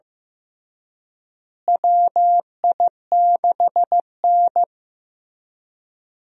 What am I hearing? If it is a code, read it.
Morse code, heard as a keyed tone, about 15 wpm, 700 Hz. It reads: WI6N
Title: USA Call Signs - 4 Characters - Repeated 3x - 15wpm